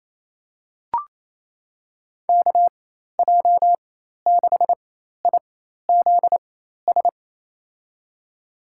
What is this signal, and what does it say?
Morse code, heard as a keyed tone, about 28 wpm, spaced as for 13 wpm, 700 Hz. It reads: KJ6IZS
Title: USA Call Signs - 4x Character Spacing - 28wpm